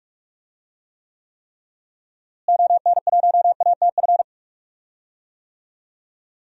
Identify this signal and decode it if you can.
Morse code, heard as a keyed tone, about 46 wpm, 700 Hz. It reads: ON1ATF
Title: DXCC-100 Call Signs - 6 Characters - Repeated 3x - 45wpm